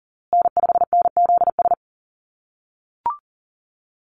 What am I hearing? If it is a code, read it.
Morse code, heard as a keyed tone, about 40 wpm, 700 Hz. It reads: N5NZS
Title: USA Call Signs Repeated 3x 40wpm